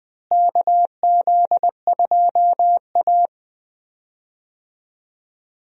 Morse code, heard as a keyed tone, about 21 wpm, 700 Hz. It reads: KZ2A
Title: USA Call Signs - 4 Characters - Repeated 3x - 20wpm